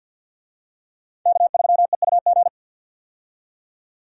Morse code, heard as a keyed tone, about 51 wpm, 700 Hz. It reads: K3EUG
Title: USA Call Signs - 5 Characters - 50wpm